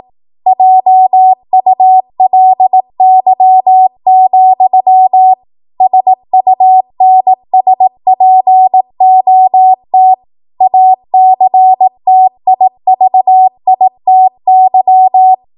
Morse code, heard as a keyed tone, about 18 wpm, 750 Hz. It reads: JULY, SUNSPOT ACTIVITY
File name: Morse / 18WPM